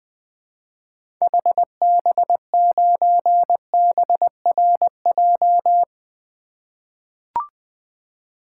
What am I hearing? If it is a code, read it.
Morse code, heard as a keyed tone, about 20 wpm, 700 Hz. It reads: HB9BRJ
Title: 2023 SOTA Activator Call Signs - 20wpm